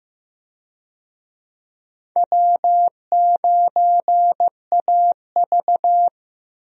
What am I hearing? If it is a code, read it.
Morse code, heard as a keyed tone, about 15 wpm, 700 Hz. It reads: W9AV
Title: USA Call Signs - 4 Characters - Repeated 2x - 15wpm